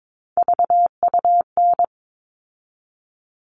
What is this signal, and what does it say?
Morse code, heard as a keyed tone, about 22 wpm, 700 Hz. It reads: VUN
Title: All Cut Numbers - 3-Digit 22wpm